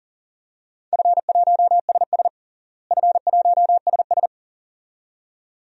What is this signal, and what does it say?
Morse code, heard as a keyed tone, about 37 wpm, 700 Hz. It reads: F1SS F1SS
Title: DXCC-100 Call Signs - 4 Characters - Repeated 2x - 40wpm